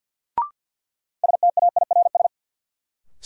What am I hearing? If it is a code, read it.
Morse code, heard as a keyed tone, about 50 wpm, 700 Hz. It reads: STAIRS
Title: Sets of 5 Words Encore - New Word Drill - 50wpm